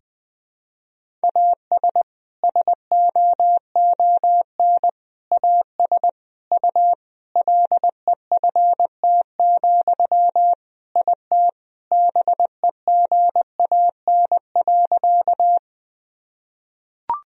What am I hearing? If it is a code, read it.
Morse code, heard as a keyed tone, about 20 wpm, 700 Hz. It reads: AS SOON AS U LEFT, IT BEGAN.
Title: Sentences from Top 400 Words and Abbreviations 20wpm